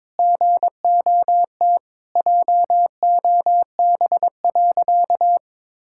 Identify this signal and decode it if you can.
Morse code, heard as a keyed tone, about 22 wpm, 700 Hz. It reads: GOT JOB.
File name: practice / 22wpm / Sentences with 4-Word Phrases - 22wpm